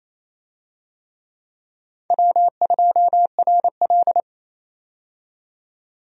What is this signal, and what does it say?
Morse code, heard as a keyed tone, about 28 wpm, 700 Hz. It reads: W2RL
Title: USA Call Signs - 4 Characters - 28wpm